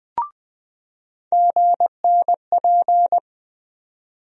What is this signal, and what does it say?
Morse code, heard as a keyed tone, about 20 wpm, 700 Hz. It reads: GNP
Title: USA Call Sign Postfixes - 3 Letters 20wpm